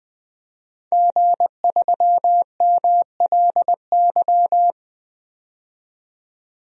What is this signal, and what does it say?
Morse code, heard as a keyed tone, about 20 wpm, 700 Hz. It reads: G3MLY
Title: DXCC-100 Call Signs - 5 Characters - 20wpm